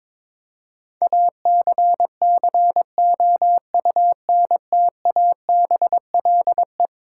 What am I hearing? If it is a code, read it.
Morse code, heard as a keyed tone, about 22 wpm, 700 Hz. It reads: ACCOUNTABLE
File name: practice / 22wpm / Eleven Letter Words - 22wpm